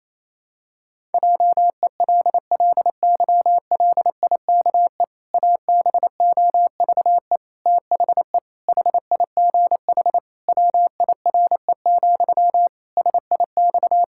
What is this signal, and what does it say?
Morse code, heard as a keyed tone, about 28 wpm, 700 Hz. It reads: JELLYLIKE ABOVE THE HIGH WIRE, SIX
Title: Pangram Challenge v2 - 28wpm